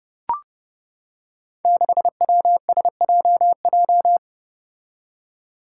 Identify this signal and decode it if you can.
Morse code, heard as a keyed tone, about 30 wpm, 700 Hz. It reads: 6WSJJ
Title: Five Challenging Letter-Number Combinations 30wpm